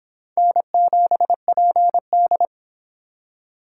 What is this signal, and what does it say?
Morse code, heard as a keyed tone, about 26 wpm, 700 Hz. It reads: N7PD